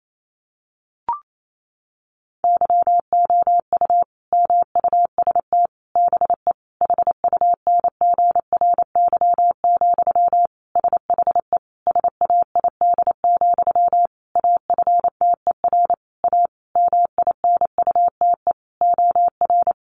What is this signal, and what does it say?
Morse code, heard as a keyed tone, about 28 wpm, 700 Hz. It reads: YOU MUST BE HUNGRY, SHE SAID, AFTER A MINUTE OR
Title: Sentences from Top 1400 Words 28wpm